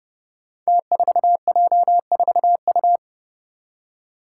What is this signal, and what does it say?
Morse code, heard as a keyed tone, about 28 wpm, 700 Hz. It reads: T4J4U
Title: Five Challenging Letter-Number Combinations 30wpm